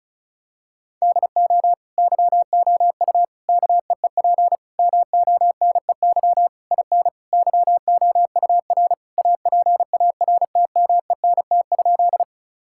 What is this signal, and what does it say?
Morse code, heard as a keyed tone, about 36 wpm, 700 Hz. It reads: DO YOU KEEP MONEY IN YOUR APARTMENT?
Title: Sentences from Top 1000 Words - Review Part 6 - 35wpm